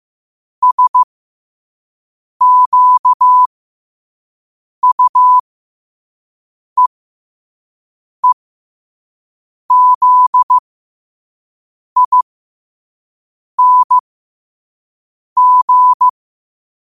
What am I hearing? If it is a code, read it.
Morse code, heard as a keyed tone, about 15 wpm, spaced as for 5 wpm, 1000 Hz. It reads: SQUEEZING
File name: qotdfiles / 5wpm